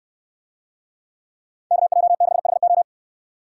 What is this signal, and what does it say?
Morse code, heard as a keyed tone, about 69 wpm, 700 Hz. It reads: KJ6HO